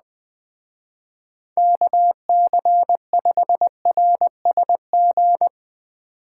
Morse code, heard as a keyed tone, about 20 wpm, 700 Hz. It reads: KC5RSG
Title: Speed Racing - USA 2x3 Call Signs 17wpm